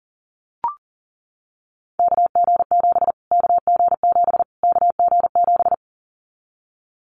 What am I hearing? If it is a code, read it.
Morse code, heard as a keyed tone, about 40 wpm, 700 Hz. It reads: KG7 KG7 KG7
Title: Callsign Prefixes for Top 25 DXCC - Repeated 3x - 40wpm